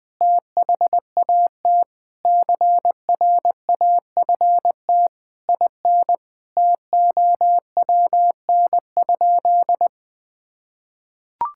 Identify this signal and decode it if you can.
Morse code, heard as a keyed tone, about 20 wpm, 700 Hz. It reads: THAT CRAFT IN TOWN?
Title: Sentences from Top 1000 Words - Review Part 6 - 20wpm